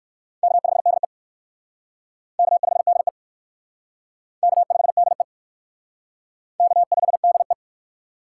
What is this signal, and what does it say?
Morse code, heard as a keyed tone, about 57 wpm, 700 Hz. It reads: K5DE K5DE K5DE K5DE